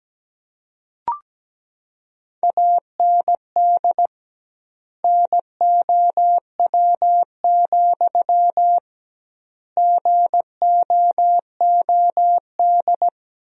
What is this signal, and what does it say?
Morse code, heard as a keyed tone, about 17 wpm, 700 Hz. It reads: AND NOW, GOOD
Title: 2x Word-Spacing - Sentences from Top 300 Words - 17wpm